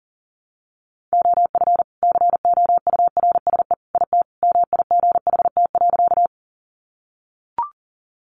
Morse code, heard as a keyed tone, about 40 wpm, 700 Hz. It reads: OF COURSE IT MIGHT.
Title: Sentences from Top 500 Words 40wpm